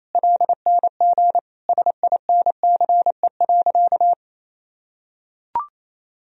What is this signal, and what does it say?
Morse code, heard as a keyed tone, about 28 wpm, 700 Hz. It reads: LNG SINCE.
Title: Sentences from Top 300 Words and Abbreviations 28wpm